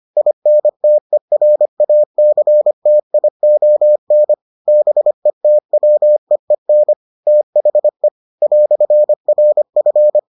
Morse code, heard as a keyed tone, about 25 wpm, 600 Hz. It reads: INTERACTION BETWEEN THE "RF